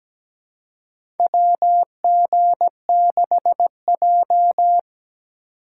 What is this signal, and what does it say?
Morse code, heard as a keyed tone, about 17 wpm, 700 Hz. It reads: WG6J
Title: USA Call Signs - 6x Character Spacing - 17wpm